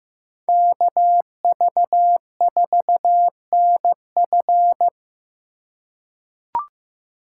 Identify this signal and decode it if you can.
Morse code, heard as a keyed tone, about 15 wpm, 700 Hz. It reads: KV4NF
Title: USA Call Signs - 5 Characters - 2x Character Spacing - 15wpm